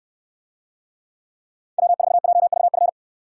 Morse code, heard as a keyed tone, about 69 wpm, 700 Hz. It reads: W4JVF